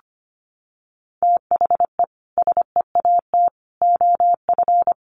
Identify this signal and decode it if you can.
Morse code, heard as a keyed tone, about 25 wpm, 700 Hz. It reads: THE SEAT OF
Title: Sentences from Top 1500 Words 25wpm